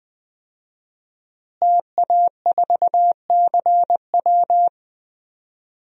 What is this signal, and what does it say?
Morse code, heard as a keyed tone, about 20 wpm, 700 Hz. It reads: TA4CW
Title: DXCC-100 Call Signs - 5 Characters - Repeated 2x - 20wpm